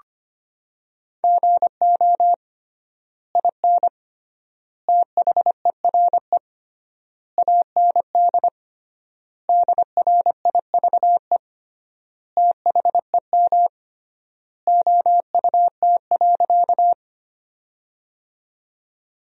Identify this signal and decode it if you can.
Morse code, heard as a keyed tone, about 25 wpm, 700 Hz. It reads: GO IN THERE AND DRIVE THEM OUT.
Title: 3x Word-Spacing - Sentences from Top 500 Words - 25wpm